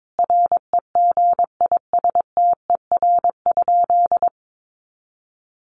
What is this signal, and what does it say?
Morse code, heard as a keyed tone, about 22 wpm, 700 Hz. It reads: REGISTER?
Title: Sentences from Top 1200 Words 22wpm